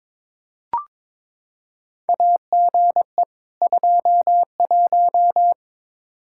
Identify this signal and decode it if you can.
Morse code, heard as a keyed tone, about 22 wpm, 700 Hz. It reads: AGE 21
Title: Practice QSO - Exchange 3 - More Info - 22wpm